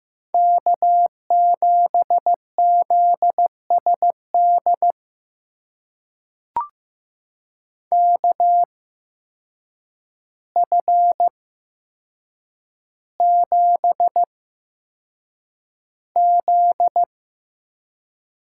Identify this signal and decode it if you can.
Morse code, heard as a keyed tone, about 15 wpm, spaced as for 7 wpm, 700 Hz. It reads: K7ZSD KF7Z
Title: USA Call Signs - 5 Characters - 8x Character Spacing - 15wpm